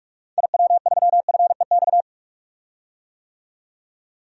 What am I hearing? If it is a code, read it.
Morse code, heard as a keyed tone, about 44 wpm, 700 Hz. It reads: IW3FEX
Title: DXCC-100 Call Signs - 6 Characters - 45wpm